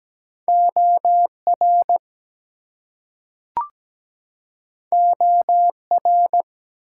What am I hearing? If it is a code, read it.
Morse code, heard as a keyed tone, about 17 wpm, 700 Hz. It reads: OR OR
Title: Two Letter Words - 17wpm